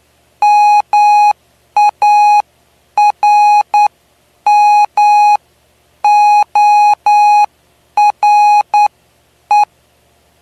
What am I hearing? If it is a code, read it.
Morse code, heard as a keyed tone, about 9 wpm, spaced as for 8 wpm, 830 Hz. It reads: MARMORE
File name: cw / py2lcd / 09-PY2LCD-Li��o 9